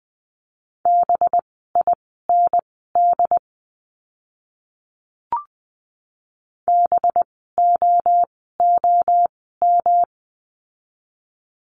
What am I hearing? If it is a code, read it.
Morse code, heard as a keyed tone, about 20 wpm, spaced as for 15 wpm, 700 Hz. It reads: BIND BOOM
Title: 185 Lesson 30f - Letter b - Words 20wpm